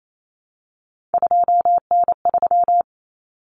Morse code, heard as a keyed tone, about 28 wpm, 700 Hz. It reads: 2N3
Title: Common Cut Numbers - 3-Digit 28wpm